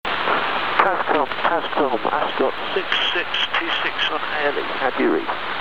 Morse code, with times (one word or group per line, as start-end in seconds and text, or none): none